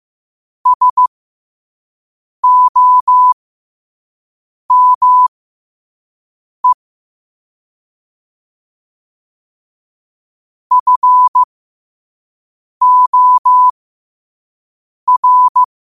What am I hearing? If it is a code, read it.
Morse code, heard as a keyed tone, about 15 wpm, spaced as for 5 wpm, 1000 Hz. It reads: SOME FOR
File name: qotdfiles / 5wpm